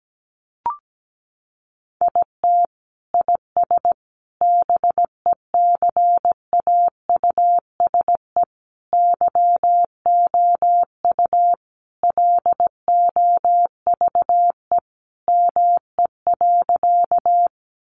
Morse code, heard as a keyed tone, about 17 wpm, 700 Hz. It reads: IT IS BECAUSE YOU LOVE ME.
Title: Sentences from Top 600 Words 17wpm